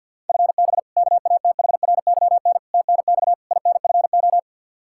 Morse code, heard as a keyed tone, about 47 wpm, 700 Hz. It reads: FB KATHRYN TNX INFO
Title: Practice QSO - Exchange 3 - More Info - 50wpm